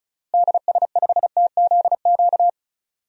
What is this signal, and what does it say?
Morse code, heard as a keyed tone, about 34 wpm, 700 Hz. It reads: DS5TZQ
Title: DXCC-100 Call Signs - 6 Characters - Repeated 3x - 35wpm